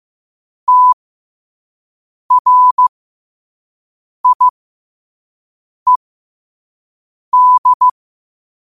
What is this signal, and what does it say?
Morse code, heard as a keyed tone, about 15 wpm, spaced as for 5 wpm, 1000 Hz. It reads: TRIED
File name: qotdfiles / 5wpm